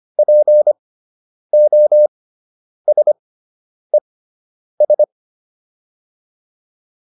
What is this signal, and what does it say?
Morse code, heard as a keyed tone, about 25 wpm, spaced as for 9 wpm, 600 Hz. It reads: POSES